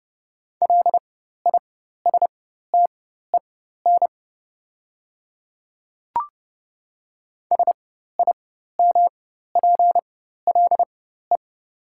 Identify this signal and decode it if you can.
Morse code, heard as a keyed tone, about 32 wpm, spaced as for 14 wpm, 700 Hz. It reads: LISTEN SIMPLE
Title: 4x Character Spacing - Top 500 Words - 30wpm